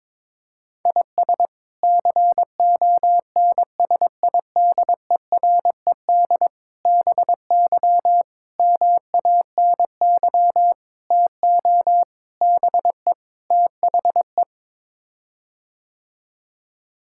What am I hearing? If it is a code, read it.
Morse code, heard as a keyed tone, about 22 wpm, 700 Hz. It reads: IS CONSIDERED BY MANY TO BE THE
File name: practice / 22wpm / Sets of 7 Words - 22wpm